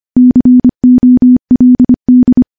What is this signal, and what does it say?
Morse code, heard as a keyed tone, about 25 wpm, 250 Hz. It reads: COLD
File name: morse-training-25WPM